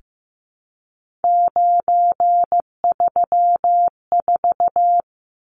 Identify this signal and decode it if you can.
Morse code, heard as a keyed tone, about 15 wpm, 700 Hz. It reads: 934
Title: Numbers - 3-Digit 15wpm